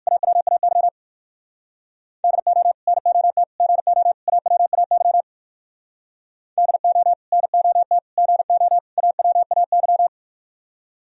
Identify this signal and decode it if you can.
Morse code, heard as a keyed tone, about 63 wpm, 700 Hz. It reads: AWAY DO NOT GO AWAY DO NOT GO AWAY